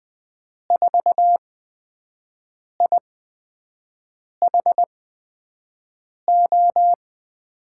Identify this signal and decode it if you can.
Morse code, heard as a keyed tone, about 20 wpm, spaced as for 5 wpm, 700 Hz. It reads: 4IHO
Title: USA Call Signs - 6 Characters - 8x Character Spacing - 20wpm